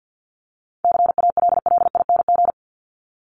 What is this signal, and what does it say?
Morse code, heard as a keyed tone, about 50 wpm, 700 Hz. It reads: CALLING
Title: ING Suffix - Top 100 Words with Base Words 50wpm